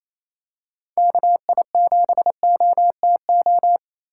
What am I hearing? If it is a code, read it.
Morse code, heard as a keyed tone, about 28 wpm, 700 Hz. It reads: KI7OTO
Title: USA Call Signs - 6 Characters - 3x Character Spacing - 28wpm